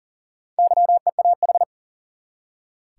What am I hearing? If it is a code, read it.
Morse code, heard as a keyed tone, about 40 wpm, 700 Hz. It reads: YEAH
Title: Four Letter Words - 40wpm